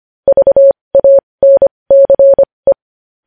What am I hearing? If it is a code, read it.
Morse code, heard as a keyed tone, about 25 wpm, spaced as for 20 wpm, 550 Hz. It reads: VANCE
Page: News Headlines in Morse Code at 20 WPM